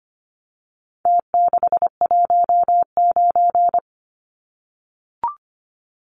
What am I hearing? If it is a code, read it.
Morse code, heard as a keyed tone, about 25 wpm, 700 Hz. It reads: T619
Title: Numbers - Contest Style - 4-Digit 25wpm